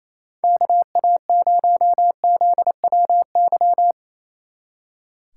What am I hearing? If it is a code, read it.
Morse code, heard as a keyed tone, about 34 wpm, 700 Hz. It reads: KA0ZWY
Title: Speed Racing - USA Call Signs 28wpm